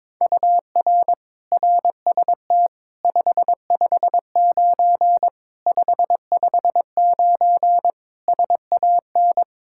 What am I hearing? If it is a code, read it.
Morse code, heard as a keyed tone, about 22 wpm, 700 Hz. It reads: UR RST 559 559 SAN
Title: Practice QSO - Exchange 1 - Beginning - 22wpm